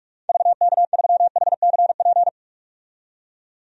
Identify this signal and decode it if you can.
Morse code, heard as a keyed tone, about 46 wpm, 700 Hz. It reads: VK3HCP
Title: DXCC-100 Call Signs - 6 Characters - 45wpm